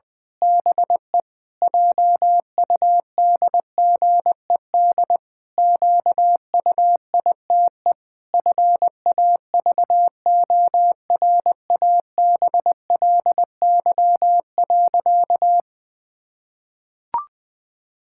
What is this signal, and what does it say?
Morse code, heard as a keyed tone, about 21 wpm, 700 Hz. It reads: BE JUDGED QUITE FAVORABLY.
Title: Pangram Challenge v2 - 20wpm